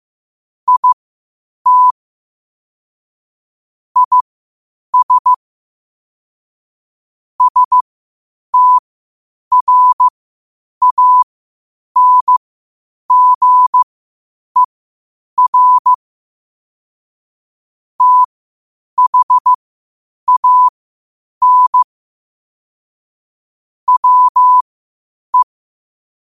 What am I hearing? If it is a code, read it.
Morse code, heard as a keyed tone, about 15 wpm, spaced as for 8 wpm, 1000 Hz. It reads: IT IS STRANGER THAN WE